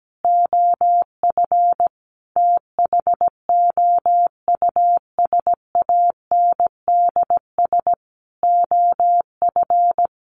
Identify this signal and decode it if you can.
Morse code, heard as a keyed tone, about 17 wpm, 700 Hz. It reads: OF THOUSANDS OF
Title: Sets of 5 Words Encore 17wpm